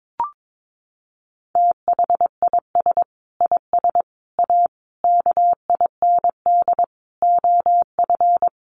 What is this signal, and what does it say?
Morse code, heard as a keyed tone, about 22 wpm, 700 Hz. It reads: THIS IS A KIND OF
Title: Sentences from Top 1500 Words 22wpm